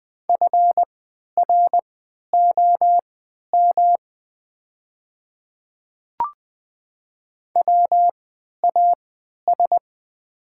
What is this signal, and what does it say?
Morse code, heard as a keyed tone, about 20 wpm, spaced as for 11 wpm, 700 Hz. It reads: FROM WAS
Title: 3x Character Spacing - Top 100 Words - 20wpm